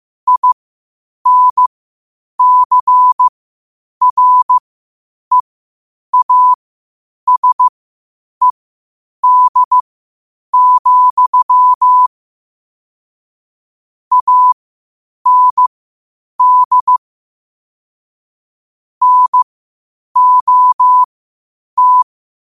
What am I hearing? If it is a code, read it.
Morse code, heard as a keyed tone, about 15 wpm, spaced as for 8 wpm, 1000 Hz. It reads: INCREASED, AND NOT
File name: qotdfiles / 8wpm